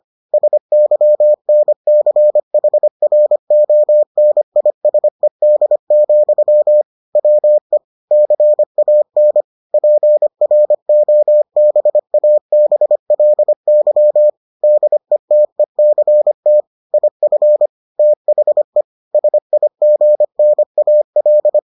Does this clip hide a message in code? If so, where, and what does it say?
Yes: Morse code from 0.3 to 21.6 s, heard as a keyed tone, about 25 wpm, 600 Hz. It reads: SYNCHRONISED, WE CAN PROBABLY DETECT IF THE SIGNAL